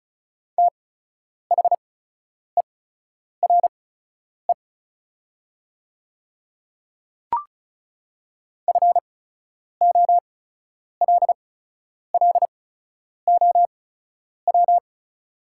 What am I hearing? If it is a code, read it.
Morse code, heard as a keyed tone, about 36 wpm, spaced as for 10 wpm, 700 Hz. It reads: THERE FOLLOW
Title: 8x Character Spacing - Top 300 Words - 35wpm